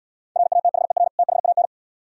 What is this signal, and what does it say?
Morse code, heard as a keyed tone, about 79 wpm, 700 Hz. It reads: RATHER THAN